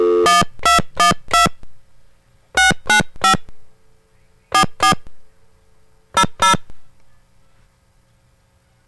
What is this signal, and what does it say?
DTMF, heard as telephone touch tones, tones of about 120 ms. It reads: 1212 695 44 00